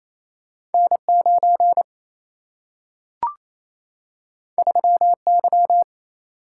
Morse code, heard as a keyed tone, about 28 wpm, 700 Hz. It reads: N9 3Y
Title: Single-Letter Single-Number 28wpm